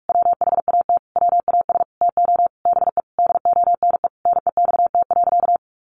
Morse code, heard as a keyed tone, about 44 wpm, 700 Hz. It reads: WHAT WAS TO BE DONE NEXT.
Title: Sentences from Top 1200 Words 45wpm